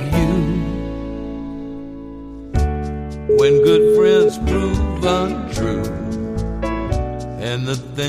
Ringback tone (Europe, 425 Hz), with noise.